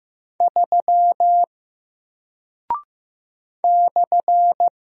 Morse code, heard as a keyed tone, about 15 wpm, 700 Hz. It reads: 3 /
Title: Single Character (Letter:Number:Punctuation:Prosign) Alt 15wpm